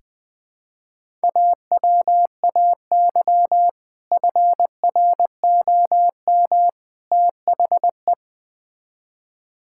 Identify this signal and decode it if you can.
Morse code, heard as a keyed tone, about 20 wpm, 700 Hz. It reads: AWAY FROM THE
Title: Build A Phrase - 3-Word Phrases - 20wpm